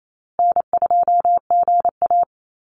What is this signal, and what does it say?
Morse code, heard as a keyed tone, about 28 wpm, 700 Hz. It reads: N2GA
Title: Easy Call Signs Repeated 3x - 28wpm